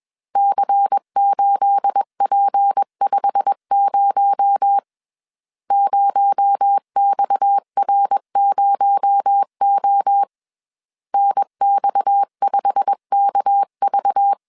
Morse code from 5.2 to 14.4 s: 0XR0O NX5KV